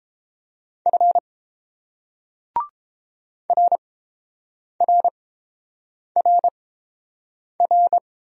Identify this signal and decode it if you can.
Morse code, heard as a keyed tone, about 31 wpm, spaced as for 7 wpm, 700 Hz. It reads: F RRRR